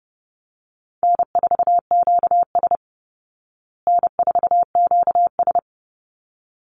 Morse code, heard as a keyed tone, about 30 wpm, 700 Hz. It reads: N4QS N4QS
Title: Easy Call Signs Repeated 2x - 30wpm